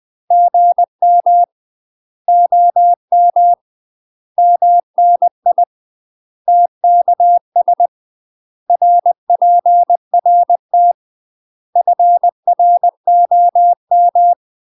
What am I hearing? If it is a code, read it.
Morse code, heard as a keyed tone, about 20 wpm, 700 Hz. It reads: GM OM MNI TKS RPRT FROM
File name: UK MASTER QSOs TRAINING/20WPM